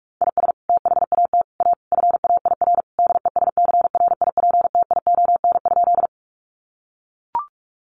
Morse code, heard as a keyed tone, about 44 wpm, 700 Hz. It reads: IS THAT A FAIR DESCRIPTION?
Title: Sentences from Top 1300 Words 45wpm